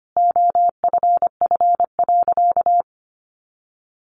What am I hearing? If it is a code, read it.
Morse code, heard as a keyed tone, about 25 wpm, 700 Hz. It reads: OFF.
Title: Sentences from Top 1000 Words 25wpm